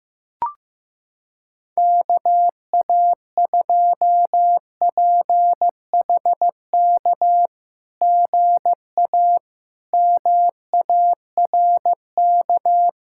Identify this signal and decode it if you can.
Morse code, heard as a keyed tone, about 15 wpm, 690 Hz. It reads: KA2PHK GA MARK
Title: SOTA QSO Practice 15wpm